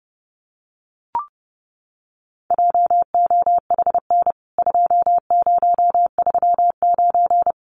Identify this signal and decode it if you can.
Morse code, heard as a keyed tone, about 30 wpm, 700 Hz. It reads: JOHN 2039
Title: CWT Exchange 30wpm